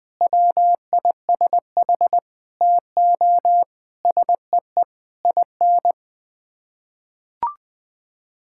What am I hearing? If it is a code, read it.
Morse code, heard as a keyed tone, about 20 wpm, 700 Hz. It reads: WISH TO SEE IN
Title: Sets of 8 Words - 20wpm